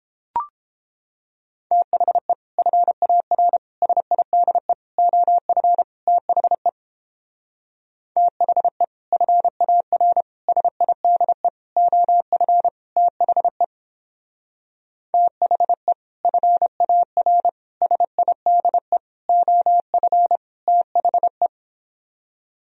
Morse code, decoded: THE FAR SIDE OF THE THE FAR SIDE OF THE THE FAR SIDE OF THE

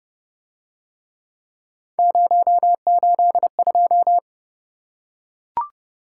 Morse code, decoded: 082

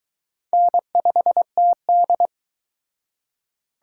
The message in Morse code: N5TD

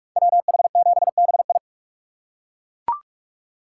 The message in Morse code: WH7BI

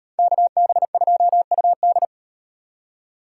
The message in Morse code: KB2UD

